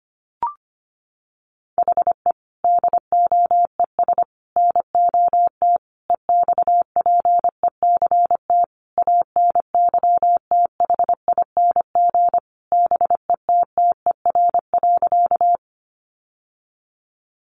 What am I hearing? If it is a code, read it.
Morse code, heard as a keyed tone, about 25 wpm, 700 Hz. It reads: HE DOES NOT EXPECT ANYTHING BETTER.
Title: Sentences from Top 1100 Words 25wpm